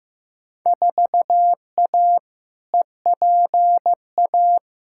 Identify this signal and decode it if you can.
Morse code, heard as a keyed tone, about 15 wpm, 700 Hz. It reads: 4A EPA
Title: Field Day Exchanges - 15wpm